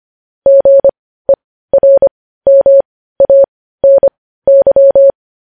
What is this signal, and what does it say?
Morse code, heard as a keyed tone, about 25 wpm, spaced as for 15 wpm, 550 Hz. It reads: GERMANY